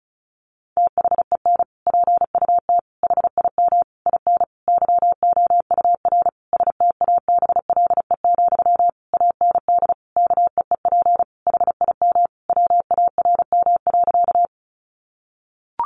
Morse code, heard as a keyed tone, about 34 wpm, 700 Hz. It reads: THEN PUT HIM IN YOUR STABLE, AND KEEP HIM WARM.
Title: Sentences from Top 1500 Words 35wpm